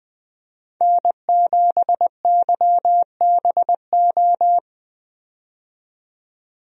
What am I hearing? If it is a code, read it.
Morse code, heard as a keyed tone, about 20 wpm, 700 Hz. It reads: N7YBO
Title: USA Call Signs - 5 Characters - 20wpm